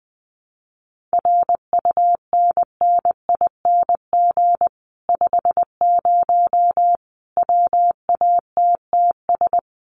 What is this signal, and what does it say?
Morse code, heard as a keyed tone, about 20 wpm, 700 Hz. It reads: RUNNING 50 WATTS
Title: Common QSO Phrases 20wpm